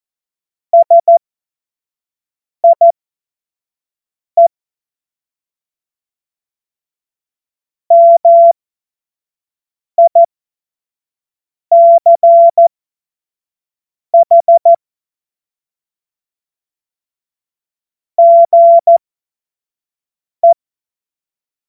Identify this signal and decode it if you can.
Morse code, heard as a keyed tone, about 14 wpm, spaced as for 5 wpm, 680 Hz. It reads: SIE MICH GE
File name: Morsekurs / Witze / Witz 16 Januar 2024 25 F70